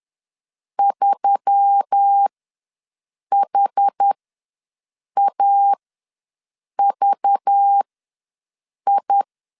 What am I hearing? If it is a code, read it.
Morse code, heard as a keyed tone, about 10 wpm, spaced as for 6 wpm, 800 Hz. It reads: 3HAVI
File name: LEZIONI / 5 LBVC / 5 LBV ws 10wpm 3645